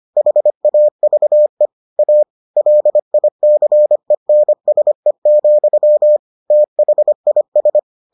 Morse code, heard as a keyed tone, about 25 wpm, 600 Hz. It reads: HAVE A LICENSE, THIS